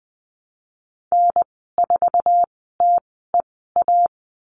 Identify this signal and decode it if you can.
Morse code, heard as a keyed tone, about 20 wpm, spaced as for 14 wpm, 700 Hz. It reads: N4TEA
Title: 062 Lesson 11g - Letter h - Call Signs 20wpm